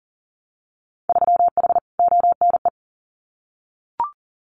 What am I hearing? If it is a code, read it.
Morse code, heard as a keyed tone, about 39 wpm, 700 Hz. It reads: 3H ONE
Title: Winter Field Day Exchanges Encore 40wpm